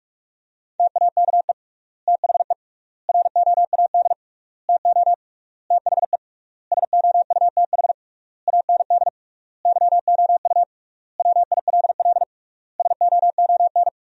Morse code, heard as a keyed tone, about 46 wpm, 700 Hz. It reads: TAKE THE ROAD TO THE SOUTH AND YOU WILL SOON
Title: 3x Word-Spacing - Sentences from Top 400 Words - 45wpm